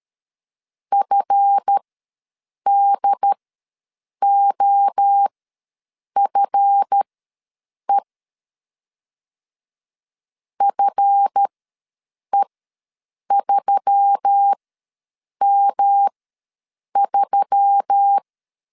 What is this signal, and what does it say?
Morse code, heard as a keyed tone, about 13 wpm, spaced as for 7 wpm, 800 Hz. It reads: FDOFE FE3M3
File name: LEZIONI / 7 GF9Z / 7 GF ws 12wpm 3140